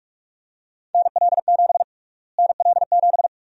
Morse code, heard as a keyed tone, about 46 wpm, 700 Hz. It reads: NL7 NL7
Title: USA Call Sign Prefixes - Repeated 2x - 45wpm